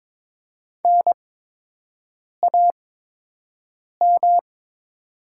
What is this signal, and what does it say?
Morse code, heard as a keyed tone, about 22 wpm, spaced as for 6 wpm, 700 Hz. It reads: NAM
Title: 8x Character Spacing - Top 300 Words - 22wpm